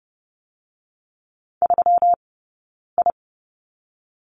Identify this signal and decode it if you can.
Morse code, heard as a keyed tone, about 30 wpm, 700 Hz. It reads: 3 I
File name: Lesson 20d - Number 3 - Two Characters 30wpm